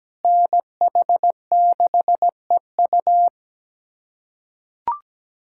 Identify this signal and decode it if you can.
Morse code, heard as a keyed tone, about 17 wpm, 700 Hz. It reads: NH6EU
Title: USA Callsign DE Callsign - 5 Characters - 17wpm